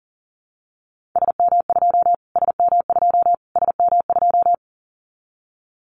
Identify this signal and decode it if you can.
Morse code, heard as a keyed tone, about 40 wpm, 700 Hz. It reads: SM2 SM2 SM2
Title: Callsign Prefixes for Top 25 DXCC - Repeated 3x - 40wpm